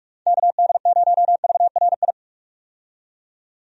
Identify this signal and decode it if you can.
Morse code, heard as a keyed tone, about 45 wpm, 700 Hz. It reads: KD0VRI
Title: USA Call Signs - 6 Characters - 45wpm